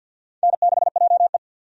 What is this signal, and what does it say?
Morse code, heard as a keyed tone, about 48 wpm, 700 Hz. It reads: N6JE
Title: USA Call Signs - 4 Characters - 50wpm